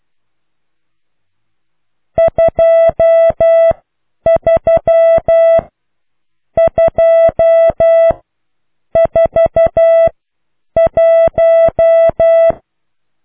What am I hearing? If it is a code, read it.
Morse code, heard as a keyed tone, about 12 wpm, spaced as for 9 wpm, 660 Hz. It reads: 23 2 4 1